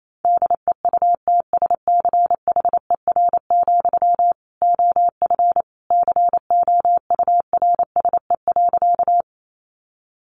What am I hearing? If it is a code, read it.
Morse code, heard as a keyed tone, about 28 wpm, 700 Hz. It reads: DEUTSCHER, OF COURSE.
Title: A Sound Of Thunder 28wpm